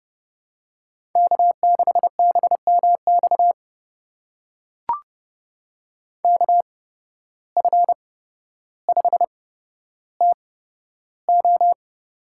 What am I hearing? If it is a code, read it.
Morse code, heard as a keyed tone, about 30 wpm, spaced as for 13 wpm, 700 Hz. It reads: K6BMX KF5TO
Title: USA Call Signs - 5 Characters - 8x Character Spacing - 30wpm